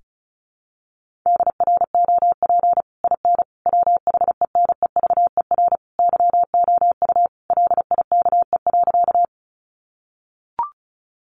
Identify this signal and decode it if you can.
Morse code, heard as a keyed tone, about 34 wpm, 700 Hz. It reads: DROP IN WHENEVER YOU LIKE.
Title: Sentences from Top 1400 Words 35wpm